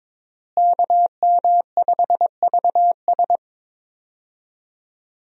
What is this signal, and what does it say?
Morse code, heard as a keyed tone, about 22 wpm, 700 Hz. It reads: KM5VS
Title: USA Call Signs - 5 Characters - Repeated 3x - 22wpm